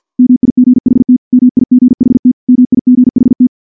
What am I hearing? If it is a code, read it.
Morse code, heard as a keyed tone, about 50 wpm, 250 Hz. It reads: MIGHT MIGHT MIGHT
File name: morse / morse-training-50WPM